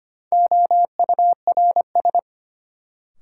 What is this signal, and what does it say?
Morse code, heard as a keyed tone, about 25 wpm, 700 Hz. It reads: OURS
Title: Sentences from Top 1300 Words - New Word Drill Alt - 25wpm